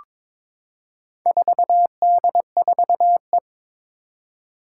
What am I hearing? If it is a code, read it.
Morse code, heard as a keyed tone, about 22 wpm, 700 Hz. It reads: 4D4E
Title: Four Challenging Letter-Number Combinations 22wpm